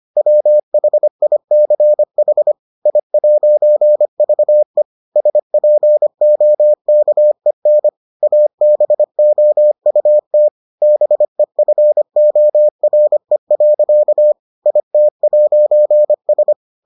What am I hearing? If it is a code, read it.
Morse code, heard as a keyed tone, about 25 wpm, 600 Hz. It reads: WHICH I'VE SPOKEN ABOUT BEFORE. IT'S